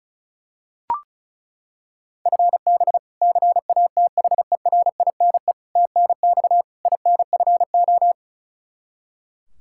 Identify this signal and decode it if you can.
Morse code, heard as a keyed tone, about 36 wpm, 700 Hz. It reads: FB CATHERINE TNX INFO
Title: K1USN SST QSO Practice 35wpm